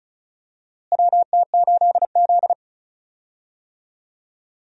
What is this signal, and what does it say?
Morse code, heard as a keyed tone, about 35 wpm, 700 Hz. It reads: WT8Z
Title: USA Call Signs - 4 Characters - Repeated 3x - 35wpm